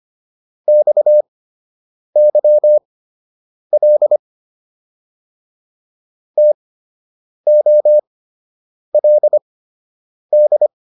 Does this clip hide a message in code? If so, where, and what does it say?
Yes: Morse code from 0.7 to 10.7 s, heard as a keyed tone, about 25 wpm, spaced as for 8 wpm, 600 Hz. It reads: XYL TOLD